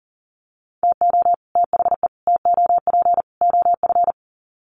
Morse code, heard as a keyed tone, about 40 wpm, 700 Hz. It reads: TO THE TOP OF
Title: Sets of 4 Words 40wpm